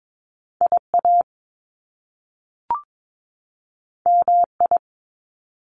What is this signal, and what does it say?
Morse code, heard as a keyed tone, about 22 wpm, 700 Hz. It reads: IA MI
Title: Top Two Letter Combinations in Words 22wpm